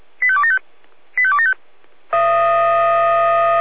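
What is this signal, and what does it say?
Five-tone selective call (ZVEI), heard as quick runs of five tones, tones of about 70 ms. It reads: 85275 85275
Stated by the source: ZVEI five-tone selective call 85352 received by SDR.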